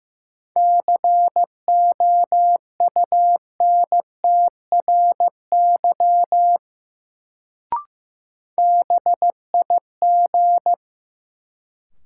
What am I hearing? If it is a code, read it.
Morse code, heard as a keyed tone, about 15 wpm, 700 Hz. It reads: COUNTRY BIG
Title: Sets of 2 Words - New Word Drill - 15wpm